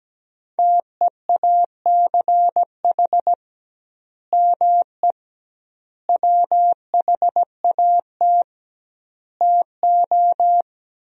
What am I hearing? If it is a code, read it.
Morse code, heard as a keyed tone, about 17 wpm, 700 Hz. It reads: TEACH ME WHAT TO
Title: 2x Word-Spacing - Sentences from Top 500 Words - 17wpm